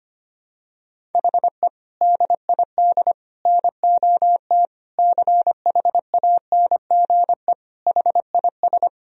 Morse code, heard as a keyed tone, about 25 wpm, 700 Hz. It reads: HE DID NOT CHANGE HIS
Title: Sentences from Top 1000 Words - Review Part 1 - 25wpm